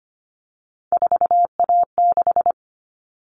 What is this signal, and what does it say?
Morse code, heard as a keyed tone, about 25 wpm, 700 Hz. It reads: 4A6
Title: All Cut Numbers - 3-Digit 25wpm